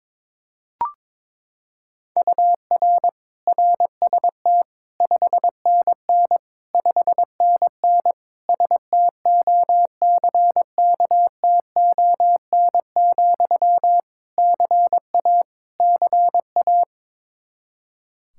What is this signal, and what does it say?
Morse code, heard as a keyed tone, about 22 wpm, 700 Hz. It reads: UR RST 5NN 5NN STOCKTON, CA CA
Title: Practice QSO - Exchange 1 - Beginning - 22wpm